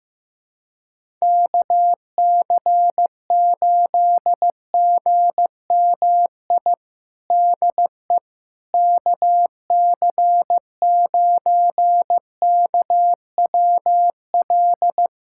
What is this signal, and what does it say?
Morse code, heard as a keyed tone, about 15 wpm, 700 Hz. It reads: KC8GMI DE KC9KWL
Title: USA Callsign DE Callsign - 6 Characters - 15wpm